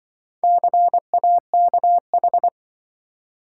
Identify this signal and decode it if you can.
Morse code, heard as a keyed tone, about 25 wpm, 700 Hz. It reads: CAKH
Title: Speed-Racing - Four Letter-Number Combinations - 20wpm